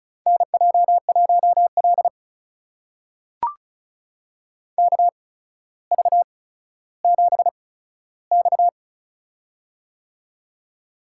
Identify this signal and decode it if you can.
Morse code, heard as a keyed tone, about 36 wpm, spaced as for 15 wpm, 700 Hz. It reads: NJ1L KV7X
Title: USA Call Signs - 4 Characters - 8x Character Spacing - 35wpm